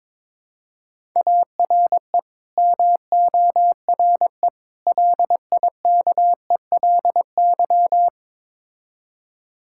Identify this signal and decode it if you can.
Morse code, heard as a keyed tone, about 22 wpm, 700 Hz. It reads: ARE MORE LIKELY
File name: practice / 22wpm / Build A Phrase - 5-Word Phrases - 22wpm